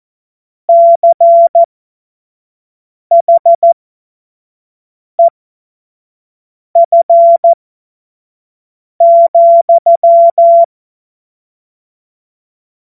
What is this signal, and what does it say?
Morse code, heard as a keyed tone, about 14 wpm, spaced as for 5 wpm, 680 Hz. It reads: CHEF,
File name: Morsekurs / Witze / Witz 12 Dezember 2023 25 F70